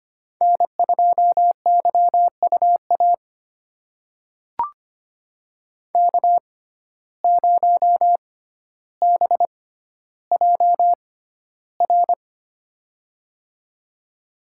Morse code, heard as a keyed tone, about 25 wpm, spaced as for 13 wpm, 700 Hz. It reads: N2YUA K0BJR
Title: USA Call Signs - 5 Characters - 6x Character Spacing - 25wpm